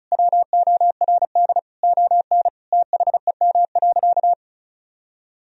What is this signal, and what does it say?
Morse code, heard as a keyed tone, about 35 wpm, 700 Hz. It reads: WORD ON THEM.
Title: Sentences from Top 1000 Words - Review Part 4 - 35wpm